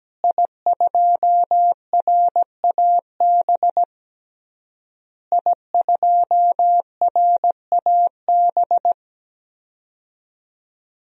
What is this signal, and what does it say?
Morse code, heard as a keyed tone, about 17 wpm, 700 Hz. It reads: I2RAB I2RAB
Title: DXCC-100 Call Signs - 5 Characters - Repeated 3x - 17wpm